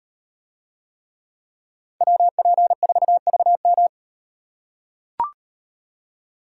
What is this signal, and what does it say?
Morse code, heard as a keyed tone, about 38 wpm, 700 Hz. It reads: WP4VM